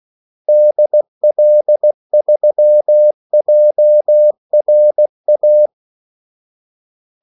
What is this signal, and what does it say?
Morse code, heard as a keyed tone, about 16 wpm, 600 Hz. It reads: DL3JRA
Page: Kaldesignaler 16 wpm